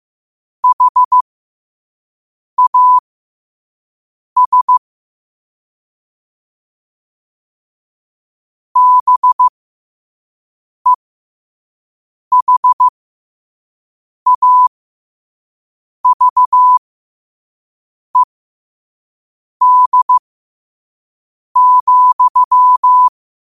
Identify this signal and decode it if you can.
Morse code, heard as a keyed tone, about 15 wpm, spaced as for 5 wpm, 1000 Hz. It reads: HAS BEHAVED,
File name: qotdfiles / 5wpm